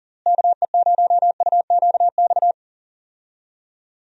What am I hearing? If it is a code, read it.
Morse code, heard as a keyed tone, about 40 wpm, 700 Hz. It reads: KE0UQX
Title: USA Call Signs - 6 Characters - Repeated 3x - 40wpm